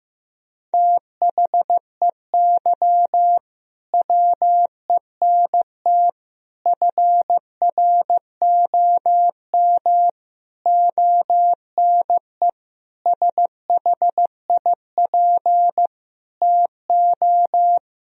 Morse code, heard as a keyed tone, about 16 wpm, 700 Hz. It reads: THEY WENT FROM ONE SHIP TO
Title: Sentences from Top 1000 Words - Review Part 2 - 15wpm